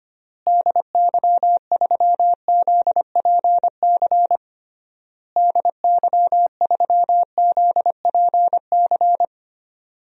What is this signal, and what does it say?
Morse code, heard as a keyed tone, about 25 wpm, 700 Hz. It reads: DY3ZPC DY3ZPC
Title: DXCC-100 Call Signs - 6 Characters - Repeated 3x - 25wpm